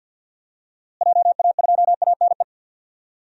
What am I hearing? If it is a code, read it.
Morse code, heard as a keyed tone, about 51 wpm, 700 Hz. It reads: JA2ANE
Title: DXCC-100 Call Signs - 6 Characters - Repeated 3x - 50wpm